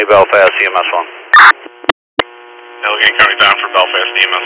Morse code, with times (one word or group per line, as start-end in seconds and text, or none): none